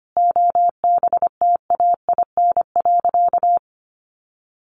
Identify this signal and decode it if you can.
Morse code, heard as a keyed tone, about 25 wpm, 700 Hz. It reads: OBTAIN.
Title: Sentences from Top 1300 Words 25wpm